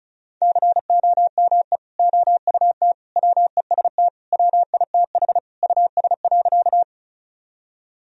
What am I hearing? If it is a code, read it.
Morse code, heard as a keyed tone, about 34 wpm, 700 Hz. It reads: COME OUT WEST WITH US.
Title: Sentences from Top 1000 Words - Review Part 2 - 35wpm